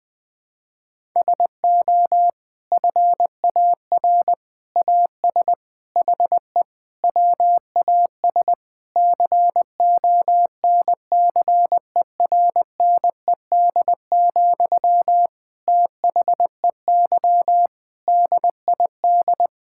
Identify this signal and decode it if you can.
Morse code, heard as a keyed tone, about 20 wpm, 700 Hz. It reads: SO FAR AS HE WAS CONCERNED, THEY DID
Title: Sentences from Top 1000 Words - Review Part 6 - 20wpm